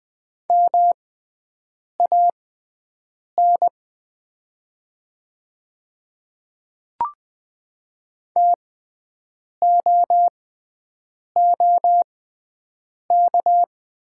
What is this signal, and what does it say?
Morse code, heard as a keyed tone, about 20 wpm, spaced as for 7 wpm, 700 Hz. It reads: MAN TOOK
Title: 6x Character Spacing - Top 500 Words - 20wpm